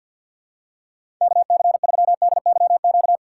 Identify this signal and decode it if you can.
Morse code, heard as a keyed tone, about 50 wpm, 700 Hz. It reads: KC3DYQ
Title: USA Call Signs - 4x Character Spacing - 50wpm